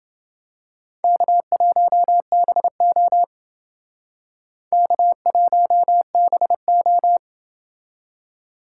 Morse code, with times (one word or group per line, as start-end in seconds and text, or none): 1.0-3.2 K1BO
4.7-7.2 K1BO